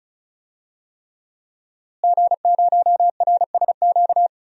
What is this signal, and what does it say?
Morse code, heard as a keyed tone, about 34 wpm, 700 Hz. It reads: G0RSQ
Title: DXCC-100 Call Signs - 5 Characters - Repeated 3x - 35wpm